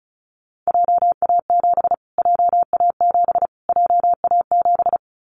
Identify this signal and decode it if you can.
Morse code, heard as a keyed tone, about 34 wpm, 700 Hz. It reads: JA7 JA7 JA7
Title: Callsign Prefixes for Top 25 DXCC - Repeated 3x - 35wpm